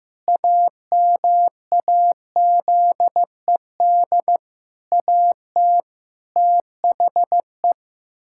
Morse code, heard as a keyed tone, about 15 wpm, 700 Hz. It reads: AMAZED AT THE
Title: Pangram Challenge v2 - 15wpm